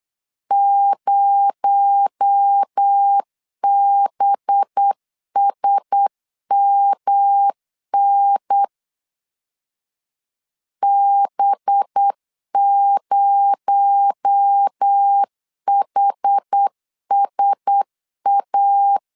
Morse code, heard as a keyed tone, about 8 wpm, 800 Hz. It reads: 0BSMN B0HSA
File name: LEZIONI / 6 PKX0 / 6 PKX0 8wpm 3341